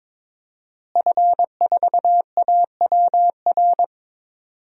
Morse code, heard as a keyed tone, about 22 wpm, 700 Hz. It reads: F4AWR
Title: DXCC-100 Call Signs - 5 Characters - Repeated 2x - 22wpm